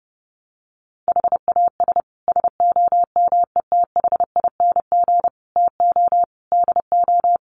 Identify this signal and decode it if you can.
Morse code, heard as a keyed tone, about 28 wpm, 700 Hz. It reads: HAS SOMETHING TO DO
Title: Sets of 5 Words Encore 30wpm